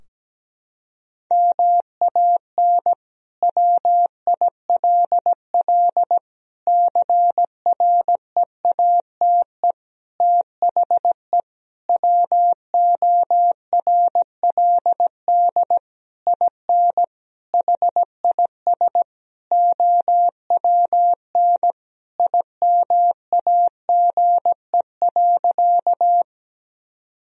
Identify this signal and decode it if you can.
Morse code, heard as a keyed tone, about 17 wpm, 700 Hz. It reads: MAN WILL CREATE THE WORLD IN HIS OWN IMAGE.
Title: Sentences from Top 1000 Words - Review Part 4 - 17wpm